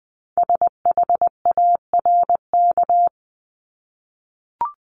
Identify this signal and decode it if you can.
Morse code, heard as a keyed tone, about 20 wpm, 700 Hz. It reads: SHARK
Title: Multi-letter Phonemes Repeated 3x 20wpm